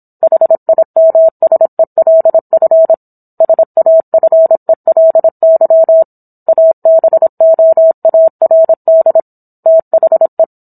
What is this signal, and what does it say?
Morse code, heard as a keyed tone, about 27 wpm, 650 Hz. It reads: HIMSELF SAFELY ABOARD THE